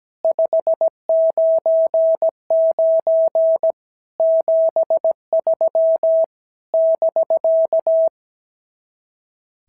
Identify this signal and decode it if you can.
Morse code, heard as a keyed tone, about 17 wpm, 650 Hz. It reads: 599 73 <BK>
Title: SOTA QSO Practice 17wpm